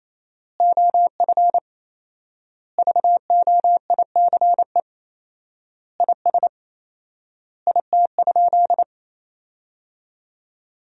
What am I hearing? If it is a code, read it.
Morse code, heard as a keyed tone, about 28 wpm, 700 Hz. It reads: OF VOICE IS IT?
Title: 4x Word-Spacing - Sentences from Top 500 Words - 28wpm